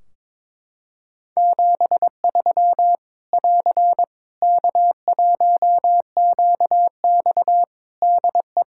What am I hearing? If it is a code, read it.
Morse code, heard as a keyed tone, about 21 wpm, 700 Hz. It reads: 73 <AR> K1QX DE
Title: Practice QSO - Exchange 4 - Ending - 22wpm